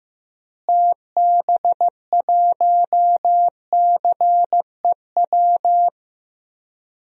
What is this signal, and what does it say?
Morse code, heard as a keyed tone, about 15 wpm, 700 Hz. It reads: TB1CEW
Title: DXCC-100 Call Signs - 6 Characters - Repeated 2x - 15wpm